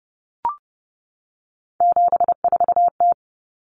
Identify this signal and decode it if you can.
Morse code, heard as a keyed tone, about 30 wpm, 700 Hz. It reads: 74T
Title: Common Cut Numbers - 3-Digit 30wpm